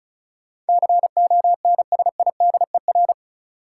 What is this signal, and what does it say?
Morse code, heard as a keyed tone, about 36 wpm, 700 Hz. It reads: CONSIDER
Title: Sentences from Top 600 Words - New Word Drill - 35wpm